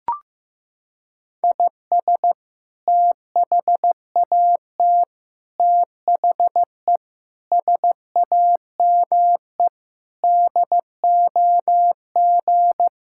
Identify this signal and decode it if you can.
Morse code, heard as a keyed tone, about 15 wpm, 700 Hz. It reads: IS THAT THE SAME DOG
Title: Sentences from Top 1600 Words 15wpm